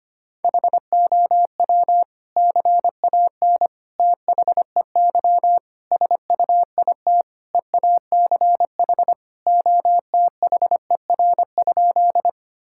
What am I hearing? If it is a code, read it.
Morse code, heard as a keyed tone, about 25 wpm, 700 Hz. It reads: HOW CAN THEY SUIT EACH OTHER?
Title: Sentences from Top 1600 Words 25wpm